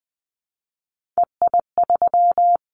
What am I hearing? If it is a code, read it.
Morse code, heard as a keyed tone, about 20 wpm, 700 Hz. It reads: EI3
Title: Callsign Prefixes for Top 50 DXCC - 20wpm